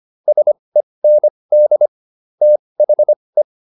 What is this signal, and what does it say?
Morse code, heard as a keyed tone, about 25 wpm, spaced as for 20 wpm, 600 Hz. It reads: SEND THE